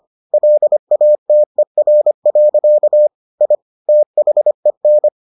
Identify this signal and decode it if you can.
Morse code, heard as a keyed tone, about 25 wpm, 600 Hz. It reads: LATER. I THEN